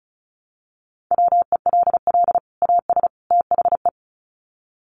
Morse code, heard as a keyed tone, about 34 wpm, 700 Hz. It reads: WELL AS THE
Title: Sets of 3 Words 35wpm